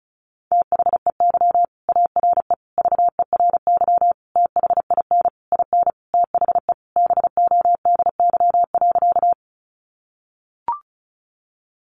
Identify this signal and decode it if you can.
Morse code, heard as a keyed tone, about 34 wpm, 700 Hz. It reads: THEY ARE VERY THIN IN THE BODY.
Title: Sentences from Top 1200 Words 35wpm